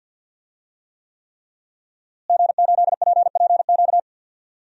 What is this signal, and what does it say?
Morse code, heard as a keyed tone, about 48 wpm, 700 Hz. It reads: G8PPQ
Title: DXCC-100 Call Signs - 5 Characters - Repeated 3x - 50wpm